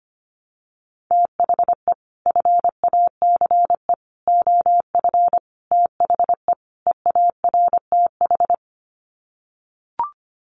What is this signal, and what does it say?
Morse code, heard as a keyed tone, about 25 wpm, 700 Hz. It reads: THE FACE OF THE EARTH
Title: Sets of 5 Words Encore 25wpm